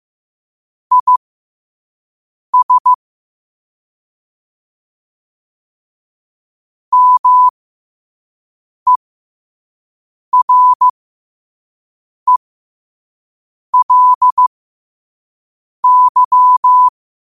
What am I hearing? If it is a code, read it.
Morse code, heard as a keyed tone, about 15 wpm, spaced as for 5 wpm, 1000 Hz. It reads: IS MERELY
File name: qotdfiles / 5wpm